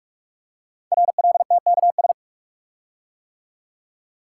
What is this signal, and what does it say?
Morse code, heard as a keyed tone, about 44 wpm, 700 Hz. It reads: RLTKS
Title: Code Groups - 5 Letters - 45wpm